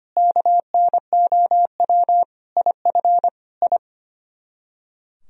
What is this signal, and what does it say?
Morse code, heard as a keyed tone, about 33 wpm, 700 Hz. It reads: KNOW IF I